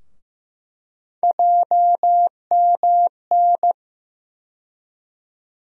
Morse code, heard as a keyed tone, about 15 wpm, 700 Hz. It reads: JMN
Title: USA Call Sign Postfixes - 3 Letters 15wpm